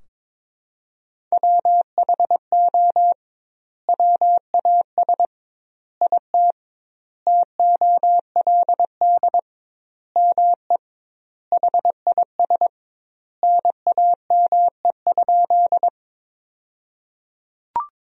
Morse code, heard as a keyed tone, about 21 wpm, 700 Hz. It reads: WHO WAS IT TOLD ME HIS NAME?
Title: 2x Word-Spacing - Sentences from Top 400 Words - 22wpm